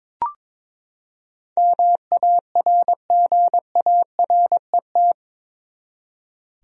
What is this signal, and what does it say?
Morse code, heard as a keyed tone, about 22 wpm, 700 Hz. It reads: MARGARET
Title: USA Common Names - 22wpm